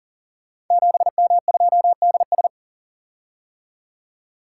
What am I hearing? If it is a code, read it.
Morse code, heard as a keyed tone, about 39 wpm, 700 Hz. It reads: 7M2DS
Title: Code Groups - 5 Letter:Numbers - 40wpm